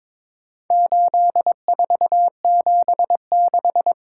Morse code, heard as a keyed tone, about 23 wpm, 700 Hz. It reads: 8476
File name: practice / 22wpm / Four Letter-Number Combinations - 22wpm